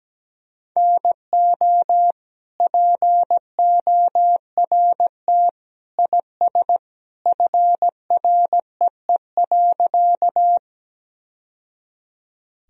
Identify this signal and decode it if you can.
Morse code, heard as a keyed tone, about 17 wpm, 700 Hz. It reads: NO PORT IS FREE.
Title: Sentences from Top 1000 Words - Review Part 1 - 17wpm